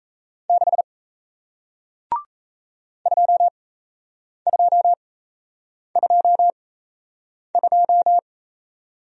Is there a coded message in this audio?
Yes: Morse code from 0.5 to 8.2 s, heard as a keyed tone, about 38 wpm, spaced as for 8 wpm, 700 Hz. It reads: 6 2222